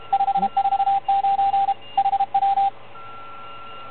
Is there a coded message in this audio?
Yes: Morse code from 0.1 to 2.7 s, heard as a keyed tone, about 33 wpm, 790 Hz, with radio noise. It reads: 549 HW